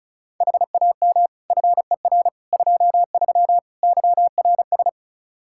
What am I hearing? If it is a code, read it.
Morse code, heard as a keyed tone, about 36 wpm, 700 Hz. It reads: HAM FER 23 YRS
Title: K1USN SST QSO Practice 35wpm